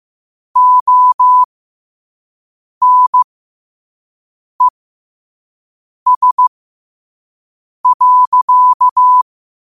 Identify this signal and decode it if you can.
Morse code, heard as a keyed tone, about 15 wpm, spaced as for 5 wpm, 1000 Hz. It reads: ONES.
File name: qotdfiles / 5wpm